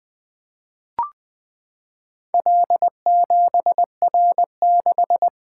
Morse code, heard as a keyed tone, about 20 wpm, 700 Hz. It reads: L7R6
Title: Four Letter-Number Combinations - 20wpm